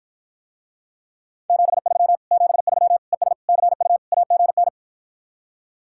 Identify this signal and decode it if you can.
Morse code, heard as a keyed tone, about 54 wpm, 700 Hz. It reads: 73 73 ES CU AGN